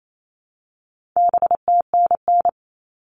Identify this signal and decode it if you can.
Morse code, heard as a keyed tone, about 28 wpm, 700 Hz. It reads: BTNN
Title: All Cut Numbers - 4-Digit 28wpm